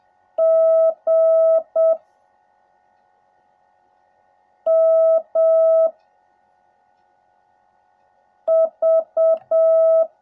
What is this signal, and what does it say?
Morse code, heard as a keyed tone, about 7 wpm, 660 Hz. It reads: G M V
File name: jh8cbh / 2021cw / cw16a